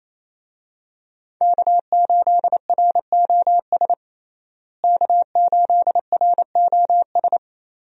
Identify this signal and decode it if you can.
Morse code, heard as a keyed tone, about 28 wpm, 700 Hz. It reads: K8ROS K8ROS
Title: USA Call Signs - 5 Characters - Repeated 3x - 28wpm